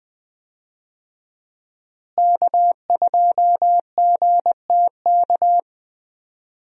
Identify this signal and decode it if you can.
Morse code, heard as a keyed tone, about 20 wpm, 700 Hz. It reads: K2GTK
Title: USA Call Signs - 5 Characters - Repeated 3x - 20wpm